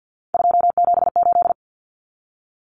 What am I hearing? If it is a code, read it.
Morse code, heard as a keyed tone, about 50 wpm, 700 Hz. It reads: 278
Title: Common Cut Numbers - 3-Digit 50wpm